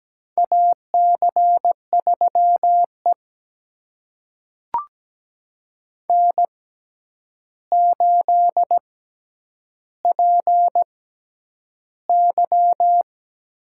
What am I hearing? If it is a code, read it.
Morse code, heard as a keyed tone, about 17 wpm, spaced as for 9 wpm, 700 Hz. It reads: AC3E N8PY
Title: USA Call Signs - 4 Characters - 6x Character Spacing - 17wpm